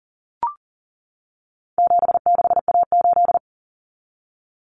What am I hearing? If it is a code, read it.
Morse code, heard as a keyed tone, about 40 wpm, 700 Hz. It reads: 76A8
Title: Common Cut Numbers - 4-Digit 40wpm